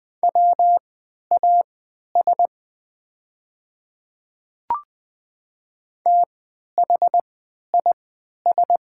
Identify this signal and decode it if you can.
Morse code, heard as a keyed tone, about 20 wpm, spaced as for 11 wpm, 690 Hz. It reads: WAS THIS
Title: 3x Character Spacing - Top 300 Words - 20wpm